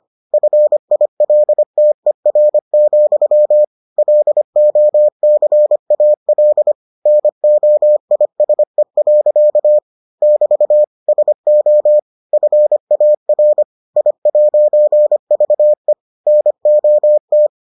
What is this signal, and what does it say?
Morse code, heard as a keyed tone, about 25 wpm, 600 Hz. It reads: FILTER, LOCAL NOISE. <BT> SO FAR I'VE NOT